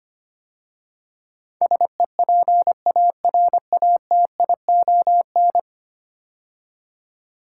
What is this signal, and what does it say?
Morse code, heard as a keyed tone, about 25 wpm, 700 Hz. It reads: SEPARATION
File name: 10 Letter Words - First 3 Letters Given - 25wpm